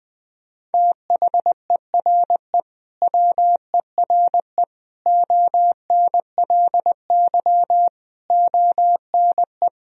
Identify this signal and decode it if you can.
Morse code, heard as a keyed tone, about 21 wpm, 700 Hz. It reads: THERE WERE ONLY ONE
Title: Sentences from Top 1000 Words - Review Part 2 - 20wpm